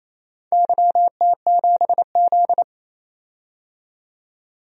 Morse code, heard as a keyed tone, about 28 wpm, 700 Hz. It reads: YT7Z
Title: DXCC-100 Call Signs - 4 Characters - 28wpm